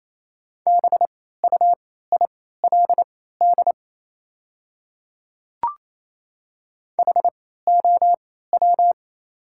Morse code, heard as a keyed tone, about 28 wpm, spaced as for 16 wpm, 700 Hz. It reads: BUILD HOW
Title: 3x Character Spacing - Top 200 Words - 28wpm